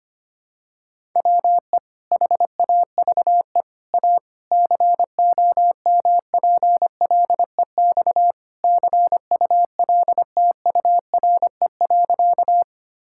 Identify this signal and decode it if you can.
Morse code, heard as a keyed tone, about 25 wpm, 700 Hz. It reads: WE HAVE A COMPLEX CULTURE.
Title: Sentences from Top 1000 Words - Review Part 6 - 25wpm